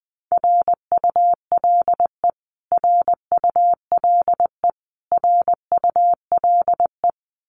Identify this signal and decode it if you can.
Morse code, heard as a keyed tone, about 20 wpm, 700 Hz. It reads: RULE RULE RULE
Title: Top 500 Words Repeated 20wpm